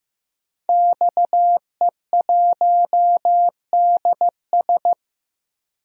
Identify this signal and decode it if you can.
Morse code, heard as a keyed tone, about 15 wpm, 700 Hz. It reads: XE1DS
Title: DXCC-100 Call Signs - 5 Characters - 15wpm